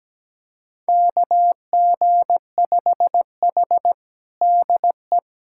Morse code, heard as a keyed tone, about 17 wpm, 700 Hz. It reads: KG5H DE
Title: USA Callsign DE Callsign - 4 Characters - 17wpm